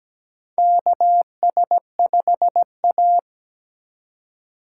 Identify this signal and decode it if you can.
Morse code, heard as a keyed tone, about 17 wpm, 700 Hz. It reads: KS5A
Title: USA Call Signs - 4 Characters - 2x Character Spacing - 17wpm